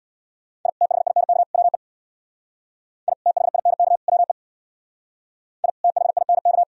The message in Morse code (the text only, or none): I THINK WE I THINK WE I THINK